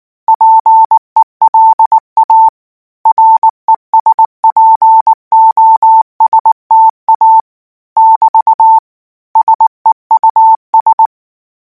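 Morse code, heard as a keyed tone, about 19 wpm, 890 Hz. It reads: PELA RESPOSTA <BT> SEUS
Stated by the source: Morse code practice text at 18 wpm, from Portal do CW.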